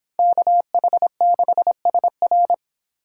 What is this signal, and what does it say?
Morse code, heard as a keyed tone, about 26 wpm, 700 Hz. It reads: KH6SR